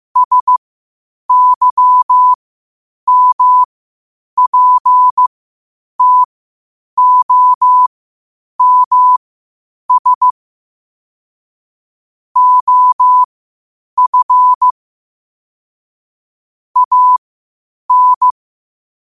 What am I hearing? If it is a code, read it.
Morse code, heard as a keyed tone, about 15 wpm, spaced as for 8 wpm, 1000 Hz. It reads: SYMPTOMS OF AN
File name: qotdfiles / 8wpm